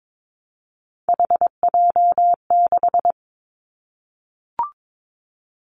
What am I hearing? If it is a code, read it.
Morse code, heard as a keyed tone, about 22 wpm, 700 Hz. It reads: HJ6
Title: Callsign Prefixes for Top 100 DXCC - 22wpm